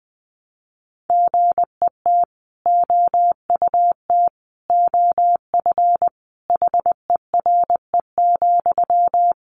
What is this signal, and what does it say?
Morse code, heard as a keyed tone, about 20 wpm, 700 Hz. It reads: GET OUT OF HERE,
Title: A Sound Of Thunder 20wpm